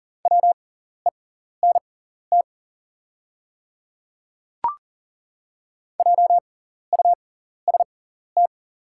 Morse code, heard as a keyed tone, about 39 wpm, spaced as for 14 wpm, 690 Hz. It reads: WENT JUST
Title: 6x Character Spacing - Top 300 Words - 40wpm